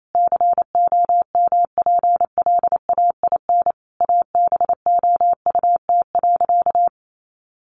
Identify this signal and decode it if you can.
Morse code, heard as a keyed tone, about 28 wpm, 700 Hz. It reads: COMPLAIN ABOUT.
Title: Sentences from Top 1300 Words 28wpm